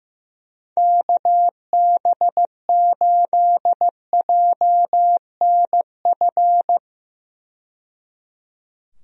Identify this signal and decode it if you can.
Morse code, heard as a keyed tone, about 15 wpm, 700 Hz. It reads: KB8JNF
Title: USA Call Signs - 6 Characters - 15wpm